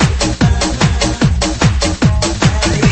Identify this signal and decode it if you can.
There is no code.